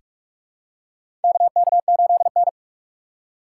Morse code, heard as a keyed tone, about 44 wpm, 700 Hz. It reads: KK8N
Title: USA Call Signs - 3x Character Spacing - 45wpm